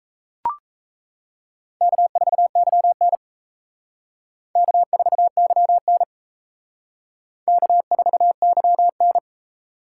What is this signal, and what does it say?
Morse code, heard as a keyed tone, about 36 wpm, 700 Hz. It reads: K4YN K4YN K4YN